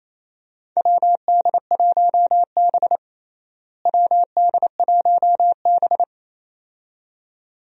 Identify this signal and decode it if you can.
Morse code, heard as a keyed tone, about 27 wpm, 700 Hz. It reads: WD1B WD1B
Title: USA Call Signs - 4 Characters - Repeated 2x - 28wpm